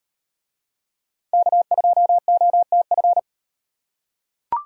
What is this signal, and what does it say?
Morse code, heard as a keyed tone, about 38 wpm, 700 Hz. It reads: K2OTF